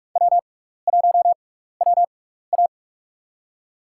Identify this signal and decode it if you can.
Morse code, heard as a keyed tone, about 44 wpm, spaced as for 15 wpm, 700 Hz. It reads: W1WA
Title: USA Call Signs - 4 Characters - 6x Character Spacing - 45wpm